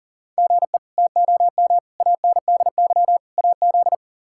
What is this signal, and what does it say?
Morse code, heard as a keyed tone, about 37 wpm, 700 Hz. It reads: GE TOM ANDY AZ
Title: K1USN SST QSO Practice - 40wpm